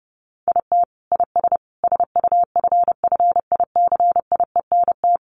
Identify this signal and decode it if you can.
Morse code, heard as a keyed tone, about 30 wpm, 700 Hz. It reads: IT IS SUFFICIENT
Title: Sentences from Top 1400 Words 30wpm